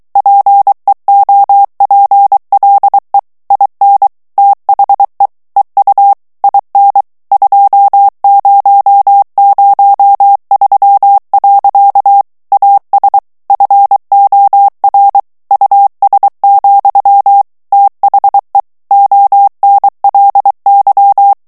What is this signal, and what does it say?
Morse code, heard as a keyed tone, about 23 wpm, 800 Hz. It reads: PEOPLE IN THE EU IN 2003. AS FOR US, THE ONLY
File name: morse mp3 / ANGLAIS texte clair 2 vitesse 22 mots minute